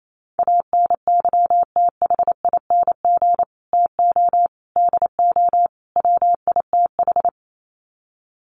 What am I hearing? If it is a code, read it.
Morse code, heard as a keyed tone, about 28 wpm, 700 Hz. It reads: ANYTHING TO DO WITH
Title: Sets of 4 Words 28wpm